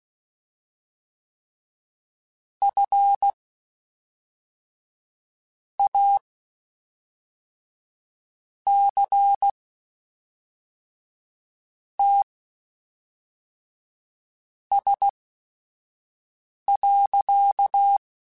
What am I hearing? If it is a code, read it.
Morse code, heard as a keyed tone, about 16 wpm, spaced as for 4 wpm, 800 Hz. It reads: FACTS.
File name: Quotes Spaced / Quotes Spaced-mp3s / IgnoringFacts sp @5